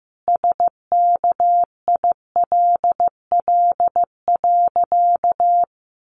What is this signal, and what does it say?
Morse code, heard as a keyed tone, about 15 wpm, 700 Hz. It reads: SKILL.
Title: Sentences from Top 1200 Words 15wpm